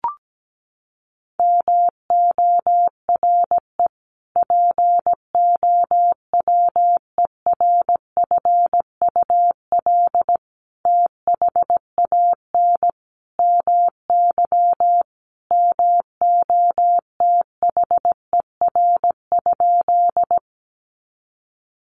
Morse code, heard as a keyed tone, about 17 wpm, 700 Hz. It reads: MORE POWERFUL THAN MY MOTHER?
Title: Sentences from Top 1300 Words 17wpm